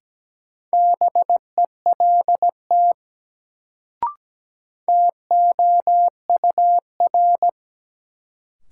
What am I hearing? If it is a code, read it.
Morse code, heard as a keyed tone, about 17 wpm, 700 Hz. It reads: BELT TOUR
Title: Sentences from Top 1400 Words - New Word Drill Alt - 17wpm